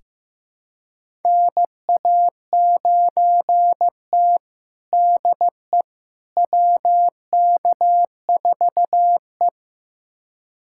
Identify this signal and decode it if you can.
Morse code, heard as a keyed tone, about 15 wpm, 700 Hz. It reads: NA9T DE WK4E
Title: USA Callsign DE Callsign - 4 Characters - 15wpm